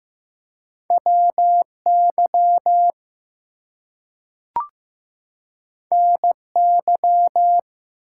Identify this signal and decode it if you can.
Morse code, heard as a keyed tone, about 15 wpm, 700 Hz. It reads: WY NY
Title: US States and Canadian Provinces 15wpm